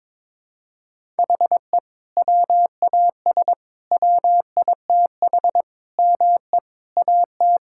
Morse code, heard as a keyed tone, about 22 wpm, 700 Hz. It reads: HE WAS WITH ME AT
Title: Sentences from Top 1000 Words - Review Part 5 - 22wpm